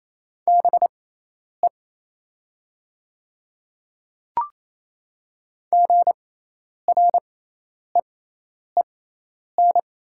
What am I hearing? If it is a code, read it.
Morse code, heard as a keyed tone, about 28 wpm, spaced as for 10 wpm, 700 Hz. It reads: BE GREEN
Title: 6x Character Spacing - Top 500 Words - 28wpm